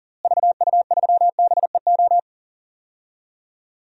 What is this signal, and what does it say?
Morse code, heard as a keyed tone, about 40 wpm, 700 Hz. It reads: VU3BEO
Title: DXCC-100 Call Signs - 6 Characters - Repeated 2x - 40wpm